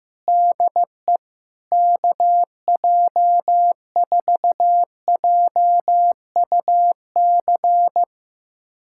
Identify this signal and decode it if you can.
Morse code, heard as a keyed tone, about 15 wpm, 700 Hz. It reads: DE KJ4JUC
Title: USA Callsign DE Callsign 15wpm